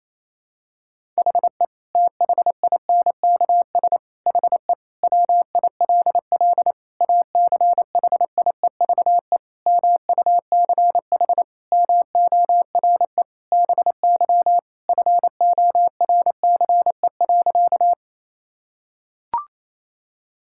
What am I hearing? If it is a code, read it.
Morse code, heard as a keyed tone, about 28 wpm, 700 Hz. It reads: HE THINKS HE WILL ACHIEVE MUCH MORE BY FORCE.
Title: Sentences from Top 1000 Words - Review Part 6 - 28wpm